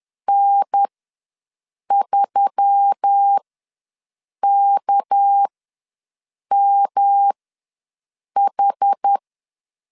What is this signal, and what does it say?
Morse code, heard as a keyed tone, about 10 wpm, spaced as for 6 wpm, 800 Hz. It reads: N3KMH
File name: LEZIONI / 6 PKX0 / 6 PK ws 10wpm 3745